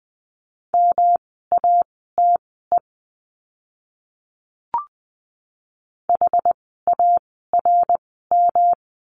Morse code, heard as a keyed tone, about 20 wpm, spaced as for 14 wpm, 700 Hz. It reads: MATE HARM
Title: 108 Lesson 18f - Letter m - Words 20wpm